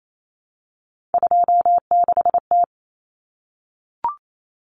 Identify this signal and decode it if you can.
Morse code, heard as a keyed tone, about 28 wpm, 700 Hz. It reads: 26T
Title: Common Cut Numbers - 3-Digit 28wpm